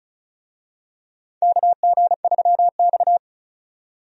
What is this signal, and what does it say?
Morse code, heard as a keyed tone, about 36 wpm, 700 Hz. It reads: KG3X
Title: USA Call Signs - 4 Characters - 35wpm